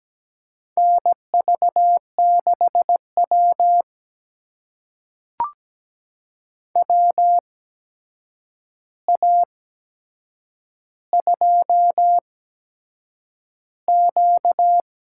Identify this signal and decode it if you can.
Morse code, heard as a keyed tone, about 17 wpm, spaced as for 7 wpm, 700 Hz. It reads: NV6W WA2Q
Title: USA Call Signs - 8x Character Spacing - 17wpm